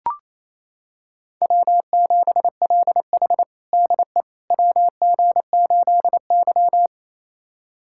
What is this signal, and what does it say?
Morse code, heard as a keyed tone, about 28 wpm, 700 Hz. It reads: W7LH DE WG8Y
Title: USA Callsign DE Callsign - 4 Characters - 28wpm